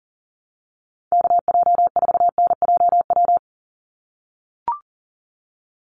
Morse code, heard as a keyed tone, about 40 wpm, 700 Hz. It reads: KJ4NJW
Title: USA Call Signs 40wpm